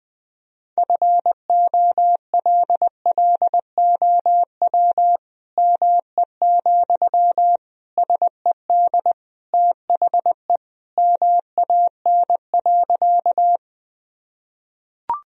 Morse code, heard as a keyed tone, about 21 wpm, 700 Hz. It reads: FOLLOW ME, SED THE MAN.
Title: Sentences from Top 200 Words and Abbreviations 20wpm